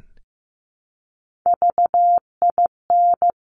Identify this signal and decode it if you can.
Morse code, heard as a keyed tone, about 15 wpm, 700 Hz. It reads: VIN
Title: Top Three Letter Combinations in Words 15wpm